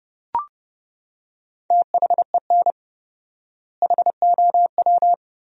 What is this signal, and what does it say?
Morse code, heard as a keyed tone, about 30 wpm, 700 Hz. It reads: THEN HOW
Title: 4x Word-Spacing - Sentences from Top 100 Words - 30wpm